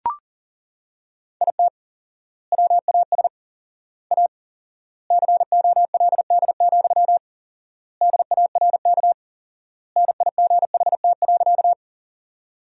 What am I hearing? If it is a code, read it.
Morse code, heard as a keyed tone, about 40 wpm, 700 Hz. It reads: IT WAS A COLD, DARK NIGHT.
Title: 4x Word-Spacing - Sentences from Top 500 Words - 40wpm